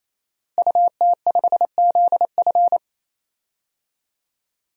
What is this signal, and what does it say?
Morse code, heard as a keyed tone, about 27 wpm, 700 Hz. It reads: UT5ZF
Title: DXCC-100 Call Signs - 5 Characters - Repeated 3x - 28wpm